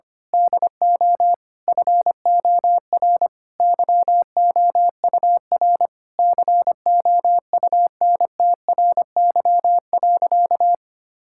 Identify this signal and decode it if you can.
Morse code, heard as a keyed tone, about 25 wpm, 700 Hz. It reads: DO FOR YOUR COUNTRY.
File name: practice / 25wpm / Sentences with 9-Word Phrases - 25wpm